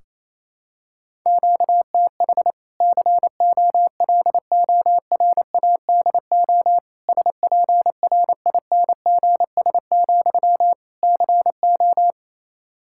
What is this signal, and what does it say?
Morse code, heard as a keyed tone, about 28 wpm, 700 Hz. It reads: QTH COLORADO SPRINGS, CO
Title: Practice QSO - Exchange 1 - Beginning - 28wpm